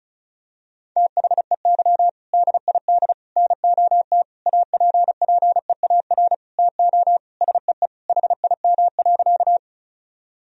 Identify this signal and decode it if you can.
Morse code, heard as a keyed tone, about 36 wpm, 700 Hz. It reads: THEY DID NOT APPEAR TO SEE HIM.
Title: Sentences from Top 1000 Words - Review Part 2 - 35wpm